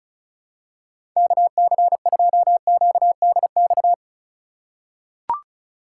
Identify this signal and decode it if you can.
Morse code, heard as a keyed tone, about 36 wpm, 700 Hz. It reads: KC2QDX
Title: USA Call Signs - 6 Characters - 3x Character Spacing - 35wpm